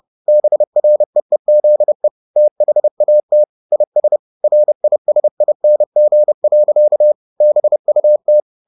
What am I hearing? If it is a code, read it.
Morse code, heard as a keyed tone, about 30 wpm, 600 Hz. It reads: BREEZE THAT IS RISING. BUT
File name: the old man and the sea / the old man and the sea 1 30wpm